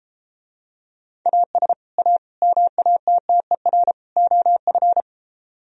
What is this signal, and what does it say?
Morse code, heard as a keyed tone, about 33 wpm, 700 Hz. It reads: AS A MATTER OF